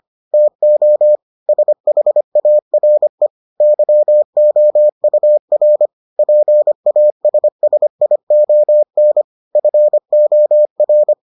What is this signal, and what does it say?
Morse code, heard as a keyed tone, about 25 wpm, 600 Hz. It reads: TO SHARE YOUR PASSION FOR